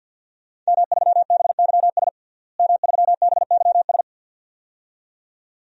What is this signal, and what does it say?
Morse code, heard as a keyed tone, about 48 wpm, 700 Hz. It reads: M3BYS M3BYS
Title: DXCC-100 Call Signs - 5 Characters - Repeated 2x - 50wpm